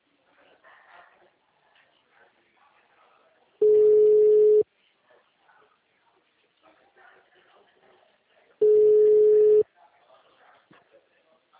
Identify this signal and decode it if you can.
Ringback tone (Europe, 425 Hz), with speech.